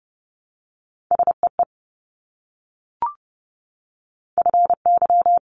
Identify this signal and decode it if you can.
Morse code, heard as a keyed tone, about 30 wpm, 700 Hz. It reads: SEE FY
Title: Multi-letter Phonemes Repeated 3x 30wpm